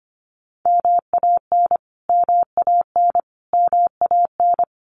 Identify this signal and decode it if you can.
Morse code, heard as a keyed tone, about 25 wpm, 700 Hz. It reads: MAN MAN MAN
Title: Top 300 Words Repeated 25wpm